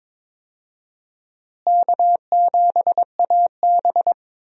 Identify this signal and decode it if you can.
Morse code, heard as a keyed tone, about 22 wpm, 700 Hz. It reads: K7AB
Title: USA Call Signs - 4 Characters - Repeated 2x - 22wpm